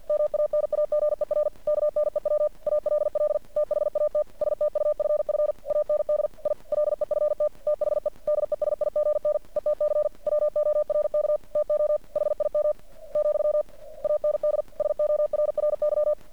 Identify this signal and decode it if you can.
Morse code, heard as a keyed tone, about 51 wpm, 600 Hz. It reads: MANAGER KNEW ALL THAT STUFF AND I LEFT THE DESIGN ETC WORK TO HIM , AND SORRY
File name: archived / CRSnet